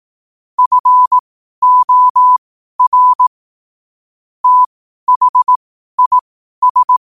Morse code, heard as a keyed tone, about 18 wpm, spaced as for 12 wpm, 1000 Hz. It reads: FOR THIS
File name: qotdfiles / 12wpm